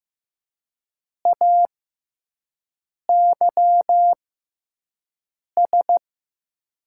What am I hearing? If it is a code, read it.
Morse code, heard as a keyed tone, about 15 wpm, spaced as for 5 wpm, 700 Hz. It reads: AYS
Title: 6x Character Spacing - Top 300 Words - 15wpm